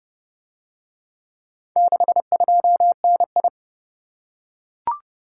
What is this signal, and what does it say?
Morse code, heard as a keyed tone, about 30 wpm, 700 Hz. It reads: 62NI